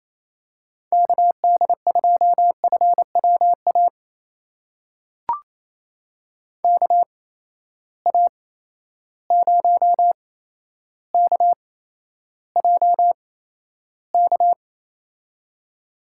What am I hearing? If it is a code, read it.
Morse code, heard as a keyed tone, about 28 wpm, spaced as for 12 wpm, 700 Hz. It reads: KD2FWA KA0KJK
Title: USA Call Signs - 6 Characters - 8x Character Spacing - 28wpm